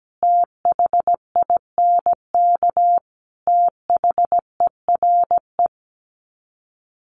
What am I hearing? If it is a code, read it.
Morse code, heard as a keyed tone, about 17 wpm, 700 Hz. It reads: THINK THERE
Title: Sets of 4 Words 17wpm